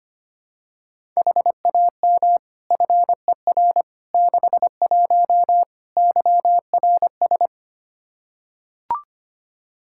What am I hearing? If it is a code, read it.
Morse code, heard as a keyed tone, about 25 wpm, 700 Hz. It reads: HAM FER 61 YRS
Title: Practice QSO - Exchange 3 - More Info - 25wpm